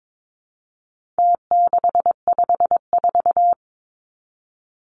Morse code, heard as a keyed tone, about 22 wpm, 700 Hz. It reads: T654
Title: Numbers - Contest Style - 4-Digit 22wpm